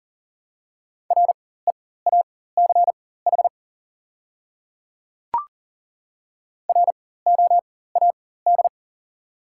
Morse code, heard as a keyed tone, about 40 wpm, spaced as for 19 wpm, 690 Hz. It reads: REACH ROAD
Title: 4x Character Spacing - Top 500 Words - 40wpm